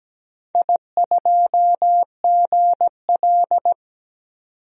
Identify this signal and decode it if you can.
Morse code, heard as a keyed tone, about 17 wpm, 700 Hz. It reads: I2GL
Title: DXCC-100 Call Signs - 4 Characters - 17wpm